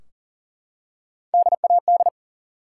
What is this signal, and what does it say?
Morse code, heard as a keyed tone, about 44 wpm, 700 Hz. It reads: DAD
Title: Three Letter Words - 40wpm